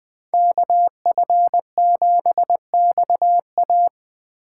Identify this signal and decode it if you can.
Morse code, heard as a keyed tone, about 23 wpm, 700 Hz. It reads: KF7XA
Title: Speed Racing - USA 2x2 Call Signs 20wpm